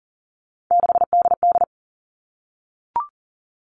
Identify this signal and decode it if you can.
Morse code, heard as a keyed tone, about 40 wpm, 700 Hz. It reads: 6DD
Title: All Cut Numbers - 3-Digit 40wpm